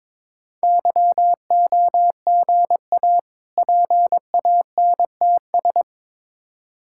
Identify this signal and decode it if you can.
Morse code, heard as a keyed tone, about 22 wpm, 690 Hz. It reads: YOGA PANTS
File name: Difficult Words with Hints - 22wpm